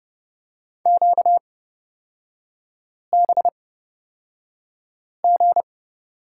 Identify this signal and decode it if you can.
Morse code, heard as a keyed tone, about 31 wpm, spaced as for 5 wpm, 690 Hz. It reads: QBG
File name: course / 30:10wpm / 236 Lesson 38e - Letter q - Three Characters 30wpm